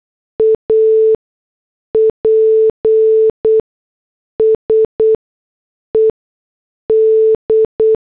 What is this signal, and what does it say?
Morse code, heard as a keyed tone, about 8 wpm, spaced as for 6 wpm, 440 Hz. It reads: APSED